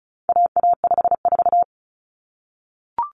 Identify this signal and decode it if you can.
Morse code, heard as a keyed tone, about 34 wpm, 700 Hz. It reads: AA54
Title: Common Cut Numbers - 4-Digit 35wpm